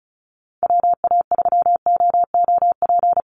Morse code, heard as a keyed tone, about 34 wpm, 700 Hz. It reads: WA3OOP
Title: USA Call Signs Repeated 3x 35wpm